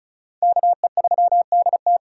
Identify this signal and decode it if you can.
Morse code, heard as a keyed tone, about 44 wpm, 700 Hz. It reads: KE3DT